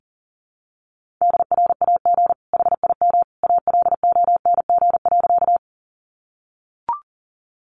Morse code, heard as a keyed tone, about 37 wpm, 700 Hz. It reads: DRAG HIM ALONG.
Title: Sentences from Top 1500 Words 40wpm